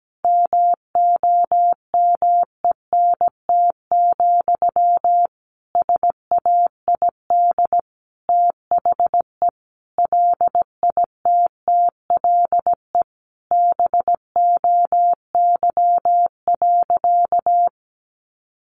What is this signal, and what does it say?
Morse code, heard as a keyed tone, about 17 wpm, 700 Hz. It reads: MOMENT, SAID THE LITTLE BOY.
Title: Sentences from Top 1000 Words 17wpm